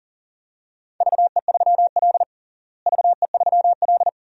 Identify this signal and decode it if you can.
Morse code, heard as a keyed tone, about 40 wpm, 700 Hz. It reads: VE3L VE3L
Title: DXCC-100 Call Signs - 4 Characters - Repeated 2x - 40wpm